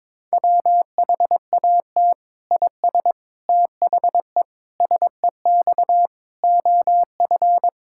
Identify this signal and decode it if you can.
Morse code, heard as a keyed tone, about 22 wpm, 700 Hz. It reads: WHAT IS THE SEX OF
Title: Sentences from Top 1600 Words 22wpm